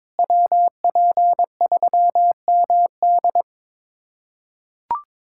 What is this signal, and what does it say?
Morse code, heard as a keyed tone, about 22 wpm, 700 Hz. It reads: WP3MD
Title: USA Callsign DE Callsign - 5 Characters - 22wpm